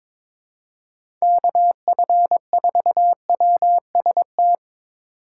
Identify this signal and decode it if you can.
Morse code, heard as a keyed tone, about 22 wpm, 700 Hz. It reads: KF4WST
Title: USA Call Signs - 6 Characters - 3x Character Spacing - 22wpm